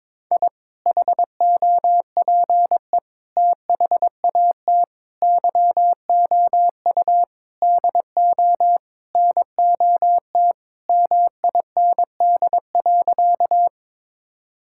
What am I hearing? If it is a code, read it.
Morse code, heard as a keyed tone, about 22 wpm, 700 Hz. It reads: I HOPE THAT YOU DO NOT MIND.
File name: practice / 22wpm / Sentences with 4-Word Phrases - 22wpm